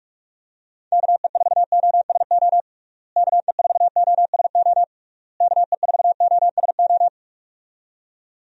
Morse code, heard as a keyed tone, about 46 wpm, 700 Hz. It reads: KE4OSO KE4OSO KE4OSO
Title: USA Call Signs - 6 Characters - Repeated 3x - 45wpm